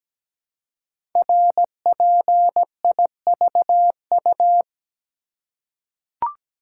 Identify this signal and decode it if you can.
Morse code, heard as a keyed tone, about 17 wpm, 700 Hz. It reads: RPIVU
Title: Code Groups - 5 Letters - 17wpm